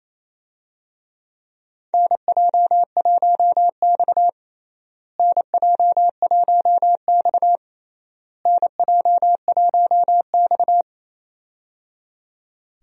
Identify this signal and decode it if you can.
Morse code, heard as a keyed tone, about 28 wpm, 700 Hz. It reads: NJ1X NJ1X NJ1X
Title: USA Call Signs - 4 Characters - Repeated 3x - 28wpm